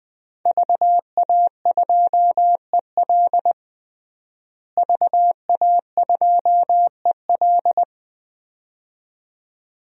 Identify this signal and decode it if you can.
Morse code, heard as a keyed tone, about 20 wpm, 700 Hz. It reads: VA2EL VA2EL
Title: DXCC-100 Call Signs - 5 Characters - Repeated 2x - 20wpm